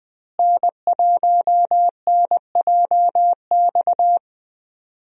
Morse code, heard as a keyed tone, about 20 wpm, 700 Hz. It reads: N1NJX
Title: USA Call Signs - 5 Characters - Repeated 2x - 20wpm